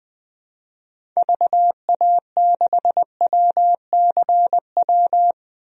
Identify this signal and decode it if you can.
Morse code, heard as a keyed tone, about 21 wpm, 700 Hz. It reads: VA6WCW
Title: DXCC-100 Call Signs - 6 Characters - 20wpm